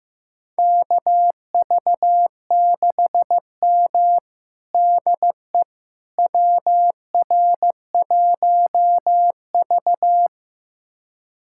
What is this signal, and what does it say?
Morse code, heard as a keyed tone, about 15 wpm, 700 Hz. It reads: KV6M DE WR1V
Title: USA Callsign DE Callsign - 4 Characters - 15wpm